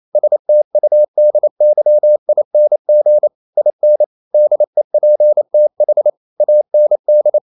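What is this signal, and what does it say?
Morse code, heard as a keyed tone, about 28 wpm, 600 Hz. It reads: STUDYING IN DEPTH AND
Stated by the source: Morse code practice at 28 wpm, text of Ken Follett's novel Never in English, from CW med Gnister.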